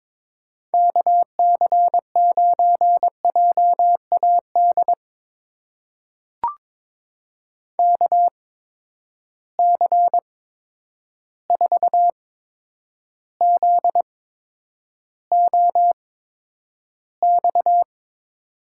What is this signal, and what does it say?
Morse code, heard as a keyed tone, about 23 wpm, spaced as for 10 wpm, 700 Hz. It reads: KC9JAD KC4ZOX
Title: USA Call Signs - 6 Characters - 8x Character Spacing - 22wpm